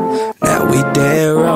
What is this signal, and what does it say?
There is no code.